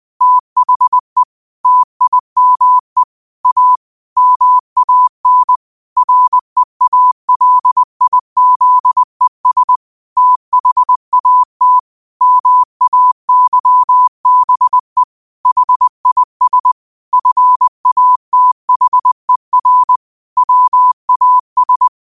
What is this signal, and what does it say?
Morse code, heard as a keyed tone, about 20 wpm, 1000 Hz. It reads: THE TIME A MAN REALIZES THAT MAYBE HIS FATHER WAS
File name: qotdfiles / 20wpm